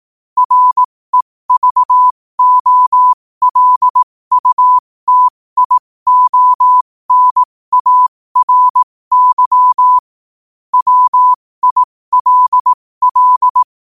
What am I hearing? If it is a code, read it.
Morse code, heard as a keyed tone, about 18 wpm, spaced as for 15 wpm, 1000 Hz. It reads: REVOLUTIONARY WILL
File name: qotdfiles / 15wpm